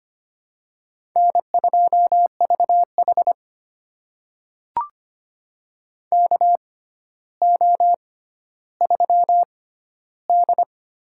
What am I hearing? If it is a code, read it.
Morse code, heard as a keyed tone, about 25 wpm, spaced as for 13 wpm, 700 Hz. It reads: N2VH KO3D
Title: USA Call Signs - 4 Characters - 6x Character Spacing - 25wpm